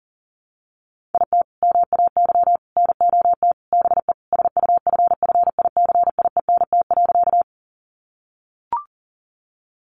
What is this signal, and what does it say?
Morse code, heard as a keyed tone, about 40 wpm, 700 Hz. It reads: IT MAY NOT BE SUFFICIENT.
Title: Sentences from Top 1400 Words 40wpm